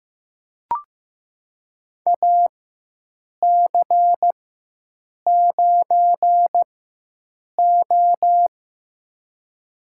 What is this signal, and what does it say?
Morse code, heard as a keyed tone, about 15 wpm, spaced as for 7 wpm, 700 Hz. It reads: AC9O
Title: USA Call Signs - 4 Characters - 4x Character Spacing - 15wpm